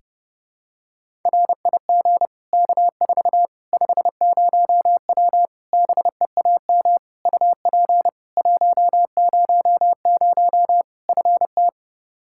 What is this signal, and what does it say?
Morse code, heard as a keyed tone, about 32 wpm, 700 Hz. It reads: RIG K4 50W BEAM UP 100 FT
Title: Practice QSO - Exchange 2 - Info - 30wpm